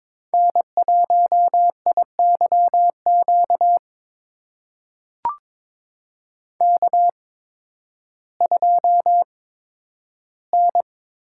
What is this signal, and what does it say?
Morse code, heard as a keyed tone, about 22 wpm, spaced as for 12 wpm, 700 Hz. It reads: N1IYQ K2N
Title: USA Call Signs - 5 Characters - 8x Character Spacing - 22wpm